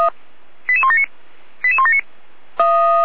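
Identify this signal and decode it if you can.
Five-tone selective call (ZVEI), heard as quick runs of five tones, tones of about 70 ms. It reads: 88179 88179
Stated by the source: ZVEI five-tone call received by SDR, archived by feuerwehr-orlamuende.org (Orlamünde fire brigade).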